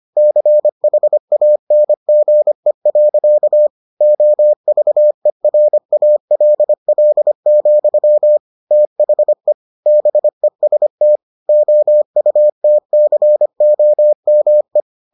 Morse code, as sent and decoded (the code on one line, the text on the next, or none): -.-. .... .- -. --. .-.-.- / --- ...- . .-. .- .-.. .-.. --..-- / - .... / -... ... - / --- ..- - -.-. --- -- .
CHANGE. OVERALL, THE BEST OUTCOME